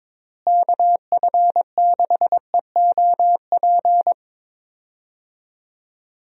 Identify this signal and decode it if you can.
Morse code, heard as a keyed tone, about 22 wpm, 700 Hz. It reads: KF6EOP
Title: USA Call Signs - 6 Characters - Repeated 3x - 22wpm